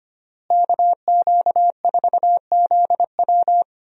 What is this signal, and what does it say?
Morse code, heard as a keyed tone, about 25 wpm, 700 Hz. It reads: KQ4ZW
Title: USA Call Signs - 5 Characters - Repeated 2x - 25wpm